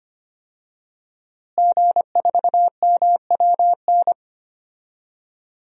Morse code, heard as a keyed tone, about 25 wpm, 700 Hz. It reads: G4MWN
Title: DXCC-100 Call Signs - 5 Characters - Repeated 3x - 25wpm